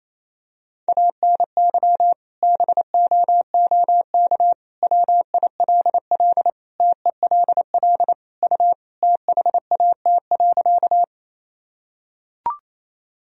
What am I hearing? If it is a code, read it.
Morse code, heard as a keyed tone, about 28 wpm, 700 Hz. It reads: ANY BOOK WILL TELL U THAT.
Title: Sentences from Top 300 Words and Abbreviations 28wpm